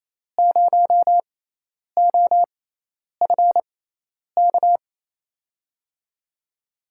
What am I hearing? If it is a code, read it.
Morse code, heard as a keyed tone, about 28 wpm, spaced as for 10 wpm, 700 Hz. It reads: 0OFK
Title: USA Call Signs - 5 Characters - 6x Character Spacing - 28wpm